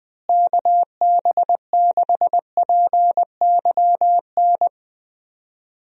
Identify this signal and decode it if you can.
Morse code, heard as a keyed tone, about 20 wpm, 700 Hz. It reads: KB6PYN
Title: USA Call Signs - 6 Characters - Repeated 2x - 20wpm